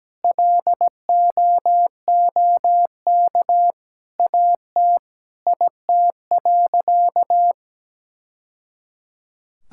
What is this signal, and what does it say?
Morse code, heard as a keyed tone, about 17 wpm, 700 Hz. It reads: LOOK AT IT.
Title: Sentences from Top 1000 Words - Review Part 5 - 17wpm